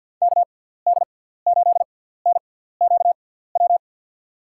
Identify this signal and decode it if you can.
Morse code, heard as a keyed tone, about 52 wpm, spaced as for 17 wpm, 700 Hz. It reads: KD8NQW
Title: USA Call Signs - 6 Characters - 6x Character Spacing - 50wpm